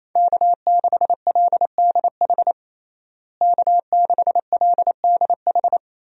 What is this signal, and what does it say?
Morse code, heard as a keyed tone, about 27 wpm, 700 Hz. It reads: K6LDH K6LDH
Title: USA Call Signs - 5 Characters - Repeated 2x - 28wpm